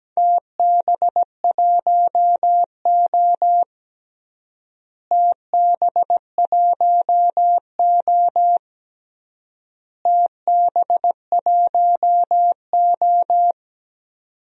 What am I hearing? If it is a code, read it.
Morse code, heard as a keyed tone, about 17 wpm, 700 Hz. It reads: TB1O TB1O TB1O
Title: DXCC-100 Call Signs - 4 Characters - Repeated 3x - 17wpm